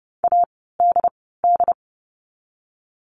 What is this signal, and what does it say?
Morse code, heard as a keyed tone, about 30 wpm, 700 Hz. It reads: A D D
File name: Lesson 16f - Letter c - Words 30wpm